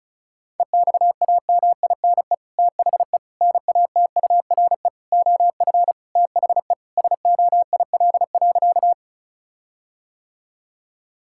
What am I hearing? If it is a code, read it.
Morse code, heard as a keyed tone, about 34 wpm, 700 Hz. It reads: EXAMINE THE NATURE OF THE SOIL.
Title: Sentences from Top 1600 Words 35wpm